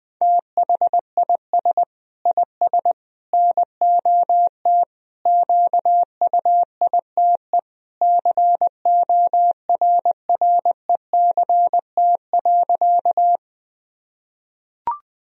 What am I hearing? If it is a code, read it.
Morse code, heard as a keyed tone, about 20 wpm, 700 Hz. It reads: THIS IS NOT QUITE CORRECT.
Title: Sentences from Top 1000 Words - Review Part 2 - 20wpm